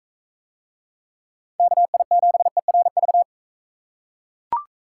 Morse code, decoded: KI7ERV

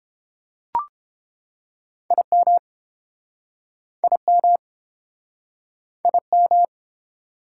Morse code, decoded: IM IM IM

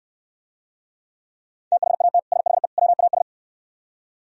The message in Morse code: THAT SHE WAS

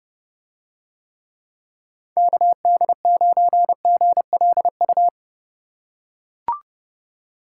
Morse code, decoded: KD9GLU

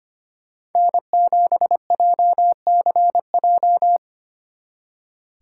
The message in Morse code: N7JCJ

